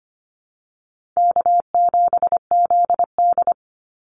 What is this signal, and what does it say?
Morse code, heard as a keyed tone, about 25 wpm, 700 Hz. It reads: K7ZD
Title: Easy Call Signs Repeated 2x - 25wpm